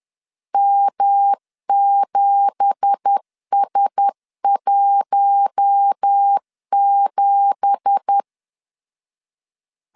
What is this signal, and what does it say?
Morse code, heard as a keyed tone, about 10 wpm, 800 Hz. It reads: M7S17
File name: LEZIONI / 9 2867 10wpm 2733